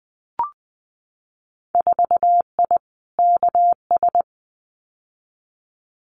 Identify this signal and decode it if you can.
Morse code, heard as a keyed tone, about 20 wpm, 700 Hz. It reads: 4I KS
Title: Winter Field Day Exchanges Encore 20wpm